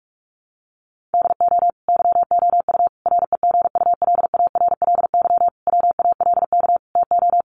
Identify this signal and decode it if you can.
Morse code, heard as a keyed tone, about 44 wpm, 700 Hz. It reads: DO YOU REGULARLY WALK TO
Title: Sentences from Top 1400 Words 45wpm